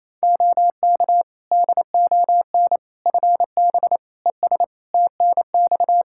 Morse code, heard as a keyed tone, about 28 wpm, 700 Hz. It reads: OK DON FB ES TNX
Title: Practice QSO - Exchange 2 - Info - 28wpm